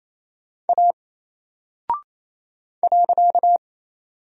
Morse code, heard as a keyed tone, about 28 wpm, 700 Hz. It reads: A .
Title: Single Character (Letter:Number:Punctuation:Prosign) Alt 28wpm